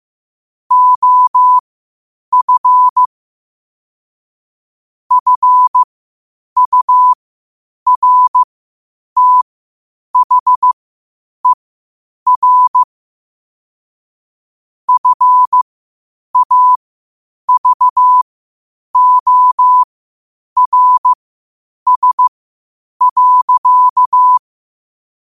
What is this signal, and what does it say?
Morse code, heard as a keyed tone, about 15 wpm, spaced as for 8 wpm, 1000 Hz. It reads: OF FURTHER FAVORS.
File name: qotdfiles / 8wpm